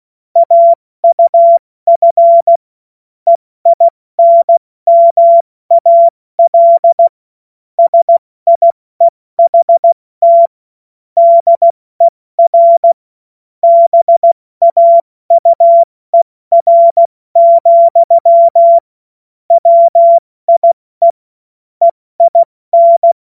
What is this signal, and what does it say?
Morse code, heard as a keyed tone, about 16 wpm, spaced as for 14 wpm, 680 Hz. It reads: AUF EINMAL SIEHT DER BAUER, WIE EIN
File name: Morsekurs / Witze / Witz 13. April 2021, 70 F80